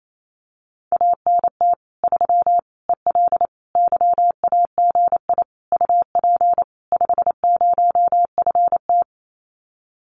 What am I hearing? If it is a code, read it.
Morse code, heard as a keyed tone, about 28 wpm, 700 Hz. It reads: ANT 3 EL YAGI UP 50FT
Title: Common QSO Phrases Encore 28wpm